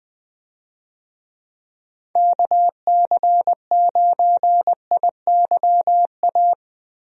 Morse code, heard as a keyed tone, about 20 wpm, 700 Hz. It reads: KC9IYA
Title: USA Call Signs - 6 Characters - Repeated 3x - 20wpm